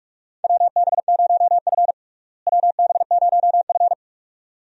Morse code, heard as a keyed tone, about 42 wpm, 700 Hz. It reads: WB0F WB0F
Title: USA Call Signs - 4 Characters - Repeated 2x - 45wpm